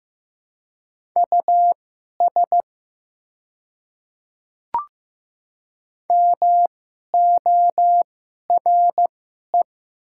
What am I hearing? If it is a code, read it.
Morse code, heard as a keyed tone, about 15 wpm, spaced as for 11 wpm, 700 Hz. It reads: US MORE
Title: 2x Character Spacing - Top 200 Words - 15wpm